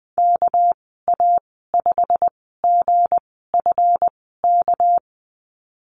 Morse code, heard as a keyed tone, about 20 wpm, spaced as for 14 wpm, 700 Hz. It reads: KA5GFK
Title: 245 Lesson 39g - Letter z - Call Signs 20wpm